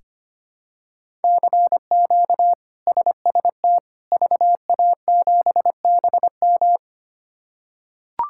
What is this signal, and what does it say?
Morse code, heard as a keyed tone, about 25 wpm, 700 Hz. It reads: CQ SST VA7BM
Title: K1USN SST QSO Practice - 25wpm